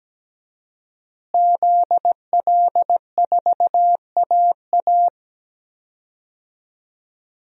Morse code, heard as a keyed tone, about 17 wpm, 700 Hz. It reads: ZL4AA
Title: DXCC-100 Call Signs - 5 Characters - Repeated 3x - 17wpm